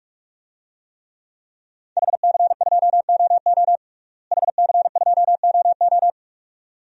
Morse code, heard as a keyed tone, about 46 wpm, 700 Hz. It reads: HC2OO HC2OO
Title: DXCC-100 Call Signs - 5 Characters - Repeated 2x - 45wpm